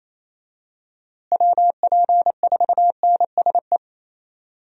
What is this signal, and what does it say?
Morse code, heard as a keyed tone, about 28 wpm, 700 Hz. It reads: WP4NSE
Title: USA Call Signs - 6 Characters - Repeated 2x - 28wpm